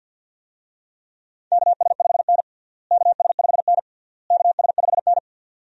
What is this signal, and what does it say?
Morse code, heard as a keyed tone, about 50 wpm, 700 Hz. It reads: KS5N KS5N KS5N
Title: USA Call Signs - 4 Characters - Repeated 3x - 50wpm